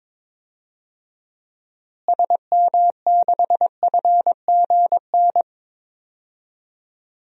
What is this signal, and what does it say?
Morse code, heard as a keyed tone, about 22 wpm, 700 Hz. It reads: SM6FGN
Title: DXCC-100 Call Signs - 6 Characters - Repeated 3x - 22wpm